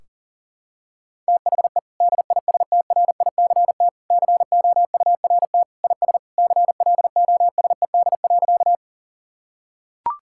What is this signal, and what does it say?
Morse code, heard as a keyed tone, about 40 wpm, 700 Hz. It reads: THE DISTRICT COURT IS CLOSED.
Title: Sentences from Top 1600 Words 40wpm